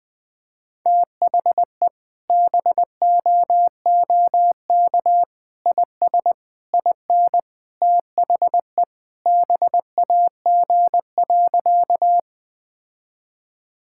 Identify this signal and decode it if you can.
Morse code, heard as a keyed tone, about 21 wpm, 700 Hz. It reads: THE BOOK IS IN THE BAG.
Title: Sentences with 3-Word Phrases - 20wpm